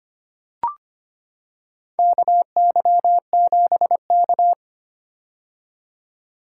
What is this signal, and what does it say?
Morse code, heard as a keyed tone, about 25 wpm, 700 Hz. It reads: KY7K
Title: K1USN SST QSO Practice - 25wpm